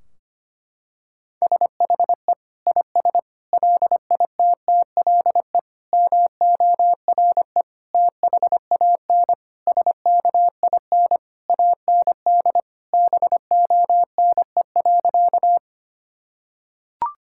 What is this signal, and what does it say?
Morse code, heard as a keyed tone, about 25 wpm, 700 Hz. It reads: SHE IS LITTLE MORE THAN SKIN AND BONE.
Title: Sentences from Top 1600 Words 25wpm